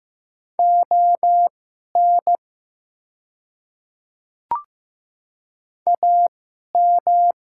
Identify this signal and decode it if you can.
Morse code, heard as a keyed tone, about 15 wpm, spaced as for 11 wpm, 700 Hz. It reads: ON AM
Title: 2x Character Spacing - Top 500 Words - 15wpm